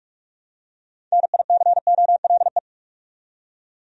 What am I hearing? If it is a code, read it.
Morse code, heard as a keyed tone, about 44 wpm, 700 Hz. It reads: NICOLE
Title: USA Common Names - 45wpm